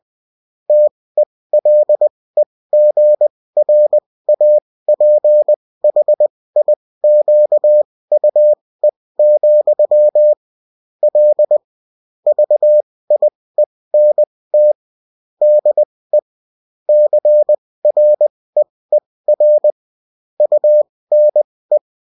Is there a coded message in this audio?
Yes: Morse code from 0.7 to 21.8 s, heard as a keyed tone, about 20 wpm, spaced as for 16 wpm, 600 Hz. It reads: TELEGRAPHIQUE, L VIENT DE CREER UNE